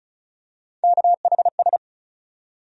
Morse code, heard as a keyed tone, about 36 wpm, 700 Hz. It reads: KHS
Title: USA Call Sign Postfixes - 3 Letters 35wpm